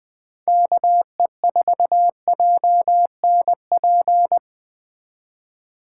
Morse code, heard as a keyed tone, about 20 wpm, 700 Hz. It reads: KE4JNP